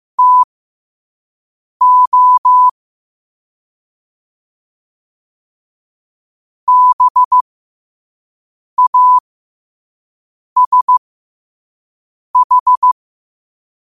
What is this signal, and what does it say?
Morse code, heard as a keyed tone, about 15 wpm, spaced as for 5 wpm, 1000 Hz. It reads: TO BASH